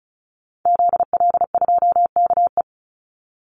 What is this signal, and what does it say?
Morse code, heard as a keyed tone, about 34 wpm, 700 Hz. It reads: ZL2KE
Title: Call Signs - 35wpm